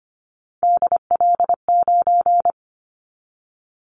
Morse code, heard as a keyed tone, about 25 wpm, 700 Hz. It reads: DL9
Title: Callsign Prefixes for Top 25 DXCC - 25wpm